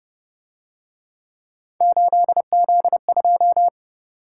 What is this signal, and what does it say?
Morse code, heard as a keyed tone, about 30 wpm, 700 Hz. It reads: 8Z2